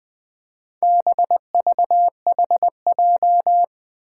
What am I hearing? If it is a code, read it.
Morse code, heard as a keyed tone, about 20 wpm, 700 Hz. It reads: BVHJ
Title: Four Challenging Letter-Number Combinations 20wpm